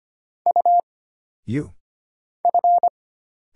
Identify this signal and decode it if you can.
Morse code, heard as a keyed tone, about 25 wpm, 700 Hz. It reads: U F
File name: Single Character (Letter:Number:Punctuation:Prosign) - Mind-Melt - 25wpm